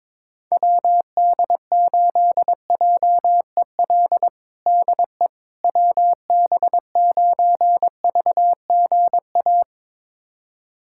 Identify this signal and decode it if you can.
Morse code, heard as a keyed tone, about 22 wpm, 700 Hz. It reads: WD8JEL DE WB9VGA
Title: USA Callsign DE Callsign - 6 Characters - 22wpm